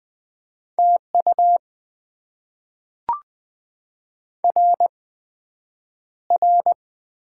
Morse code, heard as a keyed tone, about 20 wpm, spaced as for 9 wpm, 700 Hz. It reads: TU RR
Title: Top 90 QSO Elements - 8x Character Spacing - 20wpm